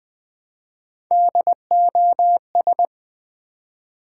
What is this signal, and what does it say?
Morse code, heard as a keyed tone, about 20 wpm, 700 Hz. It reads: DOS
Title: Build A Word - 1 Letter At A Time - 4-Letter Words - 20wpm